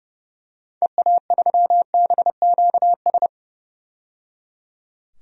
Morse code, heard as a keyed tone, about 30 wpm, 700 Hz. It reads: EA3BQS
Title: DXCC-100 Call Signs - 6 Characters - 30wpm